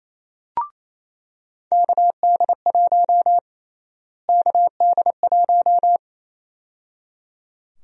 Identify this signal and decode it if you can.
Morse code, heard as a keyed tone, about 28 wpm, 700 Hz. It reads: KD1 KD1
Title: USA Call Sign Prefixes - Repeated 2x - 28wpm